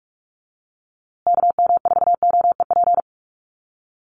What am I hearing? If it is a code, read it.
Morse code, heard as a keyed tone, about 44 wpm, 700 Hz. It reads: KM4OEP
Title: USA Call Signs Repeated 3x 45wpm